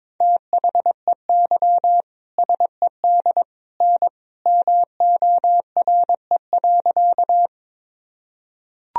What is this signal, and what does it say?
Morse code, heard as a keyed tone, about 22 wpm, 700 Hz. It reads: THEY SED N MORE.
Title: Sentences from Top 100 Words and Abbreviations 22wpm